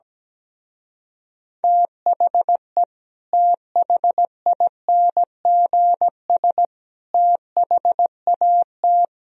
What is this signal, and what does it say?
Morse code, heard as a keyed tone, about 17 wpm, 700 Hz. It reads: THE THINGS THAT
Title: Speed-Racing - Sets of 3 Words 17wpm